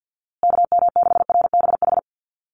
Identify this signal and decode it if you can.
Morse code, heard as a keyed tone, about 50 wpm, 700 Hz. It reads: KM6RBH
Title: USA Call Signs Repeated 3x 50wpm